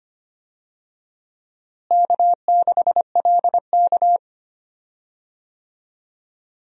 Morse code, heard as a keyed tone, about 25 wpm, 700 Hz. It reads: K6LK
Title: USA Call Signs - 4 Characters - Repeated 3x - 25wpm